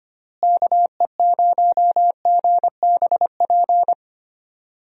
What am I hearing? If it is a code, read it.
Morse code, heard as a keyed tone, about 25 wpm, 700 Hz. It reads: KE0GBP
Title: USA Call Signs - 6 Characters - Repeated 3x - 25wpm